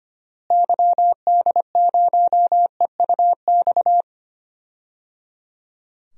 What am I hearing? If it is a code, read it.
Morse code, heard as a keyed tone, about 25 wpm, 690 Hz. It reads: YD0EUX
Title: DXCC-100 Call Signs - 6 Characters - Repeated 3x - 25wpm